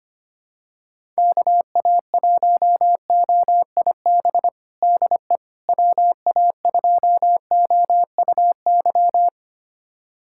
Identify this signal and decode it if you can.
Morse code, heard as a keyed tone, about 25 wpm, 700 Hz. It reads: KA1OIB DE WA2OUY
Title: USA Callsign DE Callsign 25wpm